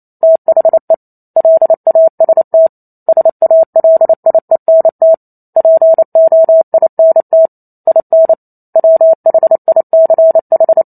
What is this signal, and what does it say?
Morse code, heard as a keyed tone, about 28 wpm, 650 Hz. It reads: THE LAST SALIENT POINT IN WHICH